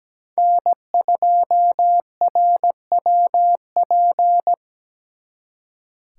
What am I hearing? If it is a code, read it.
Morse code, heard as a keyed tone, about 20 wpm, 700 Hz. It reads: N2RWP
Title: Speed Racing - USA 1x3 Call Signs 17wpm